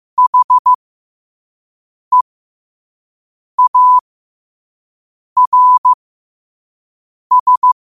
Morse code, heard as a keyed tone, about 15 wpm, spaced as for 5 wpm, 1000 Hz. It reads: HEARS
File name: qotdfiles / 5wpm